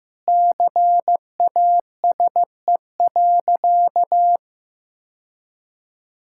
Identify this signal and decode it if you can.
Morse code, heard as a keyed tone, about 15 wpm, 700 Hz. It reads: CASE.
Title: Sentences from Top 1000 Words - Review Part 5 - 15wpm